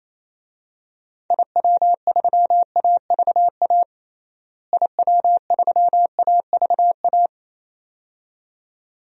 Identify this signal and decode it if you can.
Morse code, heard as a keyed tone, about 27 wpm, 700 Hz. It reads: IW3AVA IW3AVA
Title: DXCC-100 Call Signs - 6 Characters - Repeated 2x - 28wpm